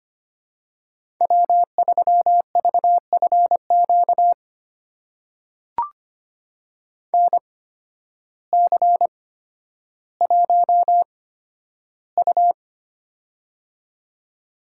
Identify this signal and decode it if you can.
Morse code, heard as a keyed tone, about 25 wpm, spaced as for 12 wpm, 700 Hz. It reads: W3VFQ NC1U
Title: USA Call Signs - 8x Character Spacing - 25wpm